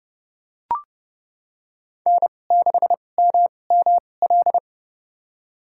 Morse code, heard as a keyed tone, about 30 wpm, spaced as for 22 wpm, 700 Hz. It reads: N6MML
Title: USA Call Signs - 5 Characters - 2x Character Spacing - 30wpm